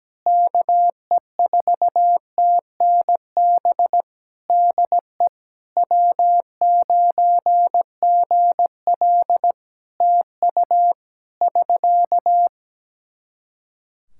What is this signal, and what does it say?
Morse code, heard as a keyed tone, about 17 wpm, 700 Hz. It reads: KE4TNB DE W9GL TU <SK>
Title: K1USN SST QSO Practice 17wpm